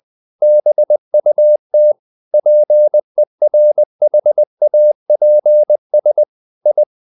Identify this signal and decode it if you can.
Morse code, heard as a keyed tone, about 20 wpm, 600 Hz. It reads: BUT PERHAPS I
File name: the old man and the sea / the old man and the sea part 1 20wpm